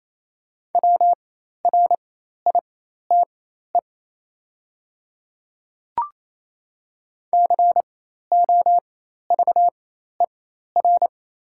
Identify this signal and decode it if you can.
Morse code, heard as a keyed tone, about 28 wpm, spaced as for 13 wpm, 690 Hz. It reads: WRITE COVER
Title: 4x Character Spacing - Top 300 Words - 28wpm